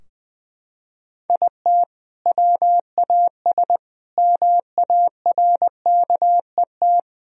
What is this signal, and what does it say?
Morse code, heard as a keyed tone, about 21 wpm, 700 Hz. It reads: IT WAS MARKET
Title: Sentences from Top 1000 Words - Review Part 3 - 20wpm